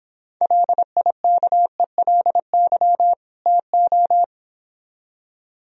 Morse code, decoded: LIKELY TO

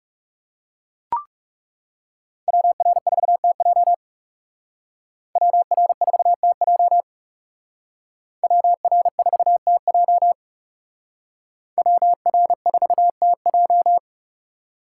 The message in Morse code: WR4TJ WR4TJ WR4TJ WR4TJ

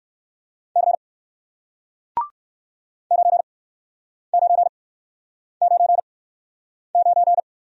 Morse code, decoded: C 9999

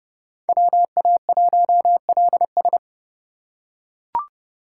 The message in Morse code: WA1LS

